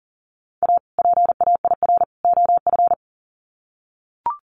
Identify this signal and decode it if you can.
Morse code, heard as a keyed tone, about 40 wpm, 700 Hz. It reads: A PAIR OF
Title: Sets of 3 Words 40wpm